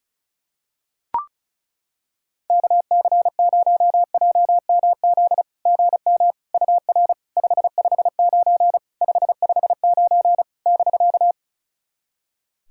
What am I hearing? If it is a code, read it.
Morse code, heard as a keyed tone, about 36 wpm, 690 Hz. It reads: KC0JMZ GM UR 559 559 <BK>
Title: SOTA QSO Practice 35wpm